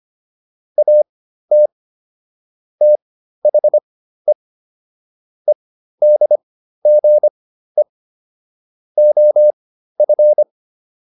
Morse code, heard as a keyed tone, about 25 wpm, spaced as for 13 wpm, 600 Hz. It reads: AT THE EDGE OF